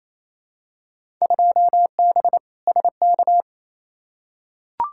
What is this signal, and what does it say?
Morse code, heard as a keyed tone, about 28 wpm, 700 Hz. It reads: 2B SK
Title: Field Day Exchanges Encore - 28wpm